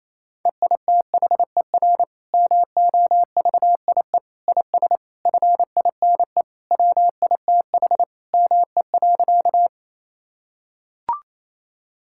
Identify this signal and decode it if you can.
Morse code, heard as a keyed tone, about 28 wpm, 700 Hz. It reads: EITHER MOVIE IS FINE WITH ME.
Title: Sentences from Top 1000 Words - Review Part 5 - 28wpm